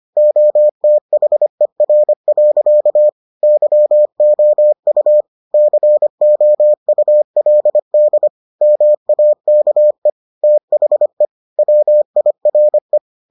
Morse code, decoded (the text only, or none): OTHER. YOU COULD MAKE THE WIRE